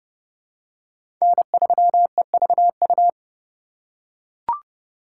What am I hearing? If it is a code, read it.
Morse code, heard as a keyed tone, about 30 wpm, 700 Hz. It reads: N3EVU
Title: USA Call Signs - 5 Characters - 4x Character Spacing - 30wpm